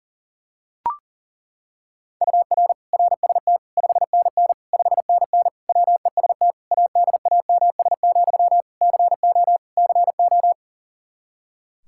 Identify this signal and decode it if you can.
Morse code, heard as a keyed tone, about 40 wpm, 700 Hz. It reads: UR RST 5NN 5NN WEST ADAMS, CO CO
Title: Practice QSO - Exchange 1 - Beginning - 40wpm